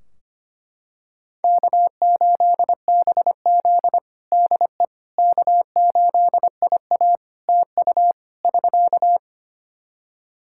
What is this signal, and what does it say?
Morse code, heard as a keyed tone, about 25 wpm, 700 Hz. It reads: K8BZ DE K8IA TU <SK>
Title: K1USN SST QSO Practice 25wpm